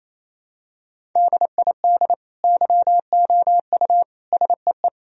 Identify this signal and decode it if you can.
Morse code, heard as a keyed tone, about 28 wpm, 700 Hz. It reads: DID YOU SEE
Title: Sentences from Top 1000 Words - Review Part 4 - 28wpm